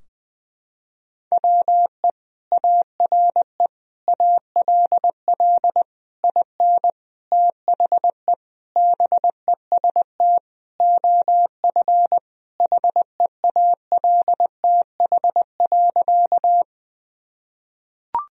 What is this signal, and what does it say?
Morse code, heard as a keyed tone, about 21 wpm, 700 Hz. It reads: WE ARE ALL IN THE BEST OF HEALTH.
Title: Sentences from Top 1000 Words - Review Part 3 - 20wpm